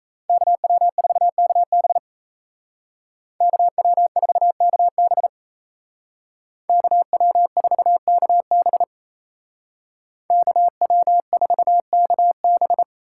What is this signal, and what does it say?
Morse code, heard as a keyed tone, about 34 wpm, 700 Hz. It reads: KW4KB KW4KB KW4KB KW4KB
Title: Speed Racing - USA Call Signs 28wpm